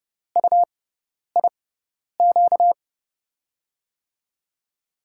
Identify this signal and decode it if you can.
Morse code, heard as a keyed tone, about 30 wpm, spaced as for 10 wpm, 700 Hz. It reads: UIQ
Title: USA Call Signs - 5 Characters - 6x Character Spacing - 30wpm